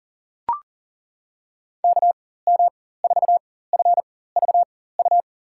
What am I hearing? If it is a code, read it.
Morse code, heard as a keyed tone, about 40 wpm, spaced as for 19 wpm, 700 Hz. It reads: KM4FVU
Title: USA Call Signs - 4x Character Spacing - 40wpm